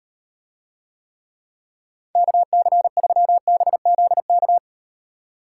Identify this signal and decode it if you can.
Morse code, heard as a keyed tone, about 38 wpm, 700 Hz. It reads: KC3BZK